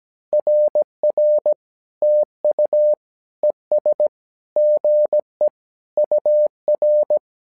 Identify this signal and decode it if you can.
Morse code, heard as a keyed tone, about 17 wpm, 600 Hz. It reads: RR TU ES GE UR
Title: SOTA QSO Practice 17wpm